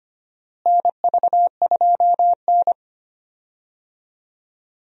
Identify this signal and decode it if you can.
Morse code, heard as a keyed tone, about 25 wpm, 700 Hz. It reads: NV2N
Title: USA Call Signs - 4 Characters - Repeated 3x - 25wpm